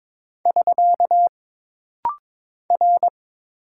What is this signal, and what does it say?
Morse code, heard as a keyed tone, about 23 wpm, 700 Hz. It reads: <SK> R
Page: This rapid-fire timing is geared to push your ICR (Instant Character Recognition) skills.